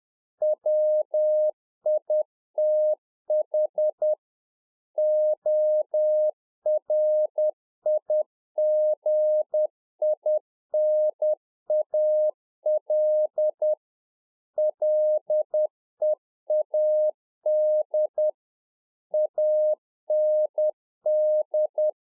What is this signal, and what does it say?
Morse code, heard as a keyed tone, about 10 wpm, 610 Hz. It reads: WITH ORIGINAL LEAD AND
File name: Czech Army Morse Keys 10WPM